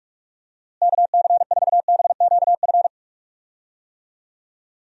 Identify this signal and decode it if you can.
Morse code, heard as a keyed tone, about 41 wpm, 700 Hz. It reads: KC4BQF
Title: USA Call Signs - 6 Characters - Repeated 3x - 45wpm